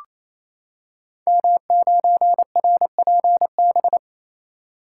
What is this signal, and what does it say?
Morse code, heard as a keyed tone, about 27 wpm, 700 Hz. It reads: M9RPB
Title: Code Groups - 5 Letter:Numbers - 28wpm